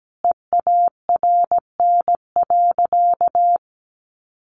Morse code, heard as a keyed tone, about 17 wpm, 700 Hz. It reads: EARN.
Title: Sentences from Top 1500 Words 17wpm